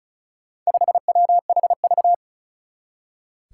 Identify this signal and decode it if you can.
Morse code, heard as a keyed tone, about 36 wpm, 700 Hz. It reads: 5WHV
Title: Four Challenging Letter-Number Combinations 35wpm